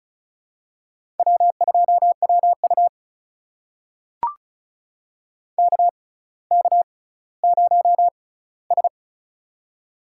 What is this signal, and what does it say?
Morse code, heard as a keyed tone, about 36 wpm, spaced as for 18 wpm, 700 Hz. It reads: W2WU KK0S
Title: USA Call Signs - 4 Characters - 6x Character Spacing - 35wpm